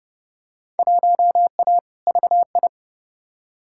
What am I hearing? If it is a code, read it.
Morse code, heard as a keyed tone, about 30 wpm, 700 Hz. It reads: 1A VI
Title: Field Day Exchanges Encore - 30wpm